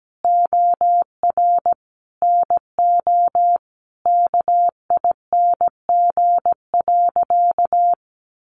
Morse code, heard as a keyed tone, about 17 wpm, 700 Hz. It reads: OR NO KING.
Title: Sentences from Top 600 Words 17wpm